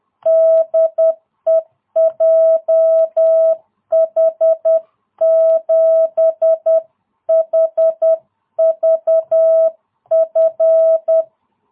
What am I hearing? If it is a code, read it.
Morse code, heard as a keyed tone, about 10 wpm, 660 Hz. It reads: DE JH7HVF